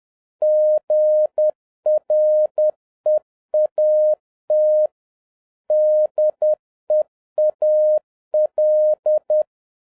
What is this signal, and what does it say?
Morse code, heard as a keyed tone, about 10 wpm, 610 Hz. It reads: GREAT DEAL
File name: 2011-07-mp3 / n21 GREETINGS FROM SINGAPORE 10wpm